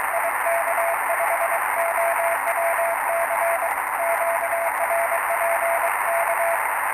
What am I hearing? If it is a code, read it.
Morse code, heard as a keyed tone, about 23 wpm, 680 Hz, with radio noise. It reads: IK5OWC QRP K
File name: attachments / 09.05.2019 14060 CQ IK5OWC R326M R309